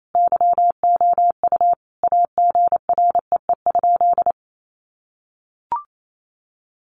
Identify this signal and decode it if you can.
Morse code, heard as a keyed tone, about 28 wpm, 700 Hz. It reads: YOU AGREE?
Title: Sentences from Top 1100 Words 28wpm